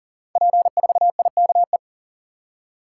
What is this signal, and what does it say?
Morse code, heard as a keyed tone, about 39 wpm, 700 Hz. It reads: P4IKE
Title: Code Groups - 5 Letter:Numbers - 40wpm